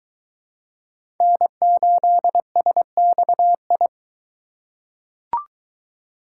Morse code, heard as a keyed tone, about 20 wpm, 700 Hz. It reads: N8SXI